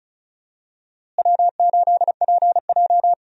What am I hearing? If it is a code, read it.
Morse code, heard as a keyed tone, about 36 wpm, 700 Hz. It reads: W8PJ
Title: USA Call Signs - 4 Characters - Repeated 3x - 35wpm